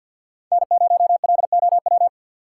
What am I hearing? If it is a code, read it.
Morse code, heard as a keyed tone, about 50 wpm, 700 Hz. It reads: N0LOW
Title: USA Call Signs - 5 Characters - 3x Character Spacing - 50wpm